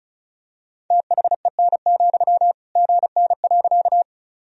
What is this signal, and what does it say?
Morse code, heard as a keyed tone, about 36 wpm, 700 Hz. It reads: THEN, GN.
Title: Sentences from Top 300 Words and Abbreviations 35wpm